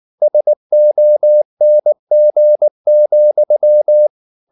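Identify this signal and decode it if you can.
Morse code, heard as a keyed tone, about 19 wpm, 600 Hz. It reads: SONG,
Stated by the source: Morse code at 19 wpm from CW med Gnister, reading Ken Follett's novel Never in English.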